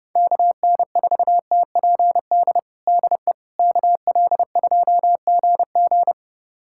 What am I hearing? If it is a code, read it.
Morse code, heard as a keyed tone, about 30 wpm, 700 Hz. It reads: KN4TPD DE KL2GG
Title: USA Callsign DE Callsign 30wpm